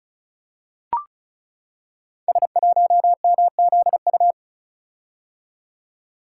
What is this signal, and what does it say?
Morse code, heard as a keyed tone, about 36 wpm, 700 Hz. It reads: S1MZU
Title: Code Groups - 5 Letter:Numbers - 35wpm